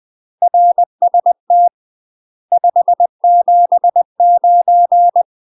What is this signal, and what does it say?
Morse code, heard as a keyed tone, about 20 wpm, 700 Hz. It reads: RST 579
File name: UK MASTER QSOs TRAINING / 20WPM / UK MASTER QSOs 0045